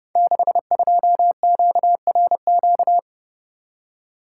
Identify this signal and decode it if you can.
Morse code, heard as a keyed tone, about 30 wpm, 700 Hz. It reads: 62QRQ
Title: Code Groups - 5 Letter:Numbers - 30wpm